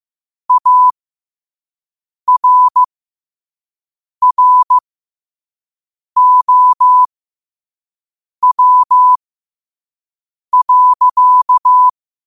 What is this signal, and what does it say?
Morse code, heard as a keyed tone, about 15 wpm, spaced as for 5 wpm, 1000 Hz. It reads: ARROW.
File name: qotdfiles / 5wpm